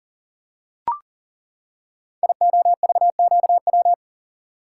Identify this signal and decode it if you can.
Morse code, heard as a keyed tone, about 40 wpm, 700 Hz. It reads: IOVQW
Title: Code Groups - 5 Letter:Numbers - 40wpm